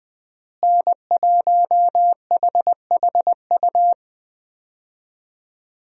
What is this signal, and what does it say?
Morse code, heard as a keyed tone, about 20 wpm, 700 Hz. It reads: N1HHU
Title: USA Call Signs - 5 Characters - Repeated 3x - 20wpm